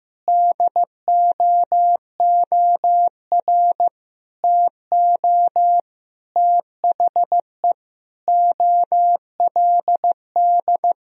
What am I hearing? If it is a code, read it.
Morse code, heard as a keyed tone, about 16 wpm, 700 Hz. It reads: DOOR TO THE OLD
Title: Sentences from Top 1000 Words - Review Part 1 - 15wpm